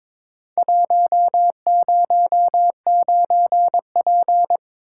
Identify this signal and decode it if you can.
Morse code, heard as a keyed tone, about 22 wpm, 700 Hz. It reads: 109P
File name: practice / 22wpm / Four Letter-Number Combinations - 22wpm